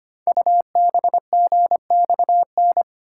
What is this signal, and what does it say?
Morse code, heard as a keyed tone, about 25 wpm, 700 Hz. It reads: UBGXN
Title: Code Groups - 5 Letters - 25wpm